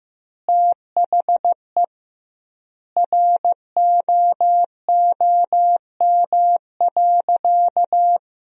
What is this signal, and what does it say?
Morse code, heard as a keyed tone, about 16 wpm, 700 Hz. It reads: THE ROOM.
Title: 2x Word-Spacing - Sentences from Top 400 Words - 15wpm